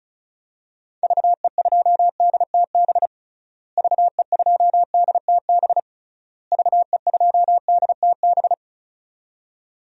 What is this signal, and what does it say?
Morse code, heard as a keyed tone, about 34 wpm, 700 Hz. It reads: VE2DTB VE2DTB VE2DTB
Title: DXCC-100 Call Signs - 6 Characters - Repeated 3x - 35wpm